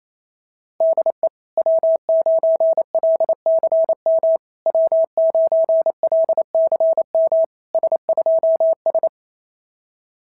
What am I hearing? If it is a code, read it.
Morse code, heard as a keyed tone, about 28 wpm, 660 Hz. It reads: DE W9LCM W9LCM S2S
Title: SOTA QSO Practice 28wpm